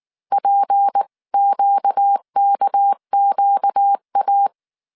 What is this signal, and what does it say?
Morse code, heard as a keyed tone, about 19 wpm, 800 Hz. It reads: PQKQA
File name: CW / DET / 1 6 1500C S C 18wpm 2356